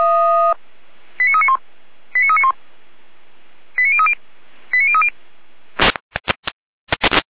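Five-tone selective call (ZVEI), heard as quick runs of five tones, tones of about 70 ms. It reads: 89391 89391 89030 89030
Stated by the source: ZVEI five-tone selective call received on SDR.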